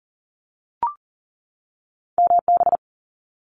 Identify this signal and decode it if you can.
Morse code, heard as a keyed tone, about 38 wpm, 700 Hz. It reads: MB
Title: Canadian Provinces and Territories 40wpm